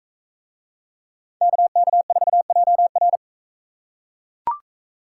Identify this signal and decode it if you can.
Morse code, heard as a keyed tone, about 42 wpm, 700 Hz. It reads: KK4JR